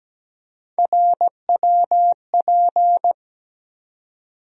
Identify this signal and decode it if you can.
Morse code, heard as a keyed tone, about 17 wpm, 700 Hz. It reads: RWP
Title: USA Call Sign Postfixes - 3 Letters 17wpm